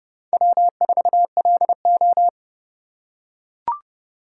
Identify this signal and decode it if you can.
Morse code, heard as a keyed tone, about 29 wpm, 700 Hz. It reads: W4LO